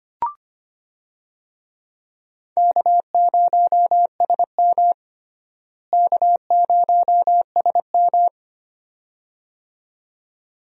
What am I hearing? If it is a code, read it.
Morse code, heard as a keyed tone, about 25 wpm, 700 Hz. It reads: K0SM K0SM
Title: USA Call Signs - 4 Characters - Repeated 2x - 25wpm